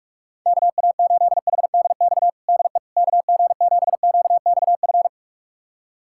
Morse code, heard as a keyed tone, about 46 wpm, 690 Hz. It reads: KA8HDX DE KG7QXF
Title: USA Callsign DE Callsign - 6 Characters - 45wpm